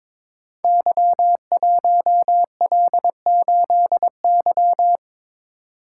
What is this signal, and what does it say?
Morse code, heard as a keyed tone, about 22 wpm, 700 Hz. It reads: Y1L8Y
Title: Code Groups - 5 Letter:Numbers - 22wpm